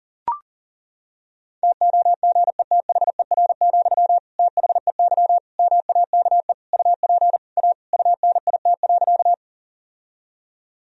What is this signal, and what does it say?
Morse code, heard as a keyed tone, about 40 wpm, 700 Hz. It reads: TOGETHER, THEY MAKE UP A UNIT.
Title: Sentences from Top 1000 Words - Review Part 2 - 40wpm